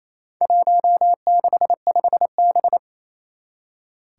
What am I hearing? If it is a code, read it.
Morse code, heard as a keyed tone, about 28 wpm, 700 Hz. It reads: 165B
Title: Four Challenging Letter-Number Combinations 28wpm